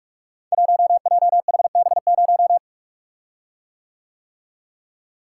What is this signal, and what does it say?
Morse code, heard as a keyed tone, about 44 wpm, 700 Hz. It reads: 1JHB0
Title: Code Groups - 5 Letter:Numbers - 45wpm